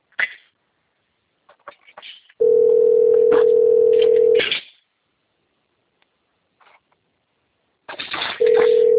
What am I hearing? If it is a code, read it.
Ringback tone (North America), 440 + 480 Hz.
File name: out- - - - - .11479